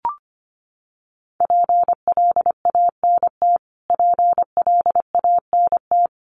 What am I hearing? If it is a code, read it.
Morse code, heard as a keyed tone, about 25 wpm, 700 Hz. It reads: PLANT PLANT
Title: Top 500 Words Repeated 25wpm